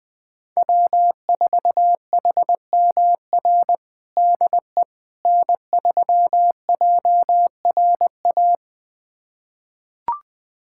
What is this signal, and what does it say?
Morse code, heard as a keyed tone, about 21 wpm, 700 Hz. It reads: W4HMR DE N3JRA
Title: USA Callsign DE Callsign - 5 Characters - 20wpm